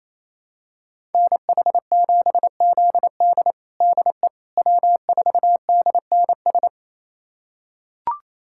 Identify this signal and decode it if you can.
Morse code, heard as a keyed tone, about 27 wpm, 700 Hz. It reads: NH7ZD DE W4DNS
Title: USA Callsign DE Callsign - 5 Characters - 28wpm